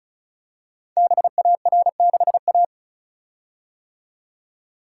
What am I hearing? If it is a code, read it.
Morse code, heard as a keyed tone, about 36 wpm, 700 Hz. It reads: BAR6A
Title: Code Groups - 5 Letter:Numbers - 35wpm